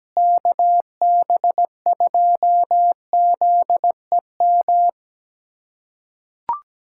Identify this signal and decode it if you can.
Morse code, heard as a keyed tone, about 17 wpm, 700 Hz. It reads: KB2ZEM
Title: USA Callsign DE Callsign - 6 Characters - 17wpm